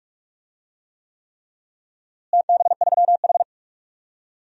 Morse code, heard as a keyed tone, about 44 wpm, 700 Hz. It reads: TB3H
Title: DXCC-100 Call Signs - 4 Characters - Repeated 3x - 45wpm